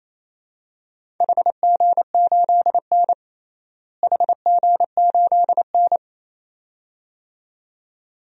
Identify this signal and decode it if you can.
Morse code, heard as a keyed tone, about 28 wpm, 700 Hz. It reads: HG8N HG8N
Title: DXCC-100 Call Signs - 4 Characters - Repeated 2x - 28wpm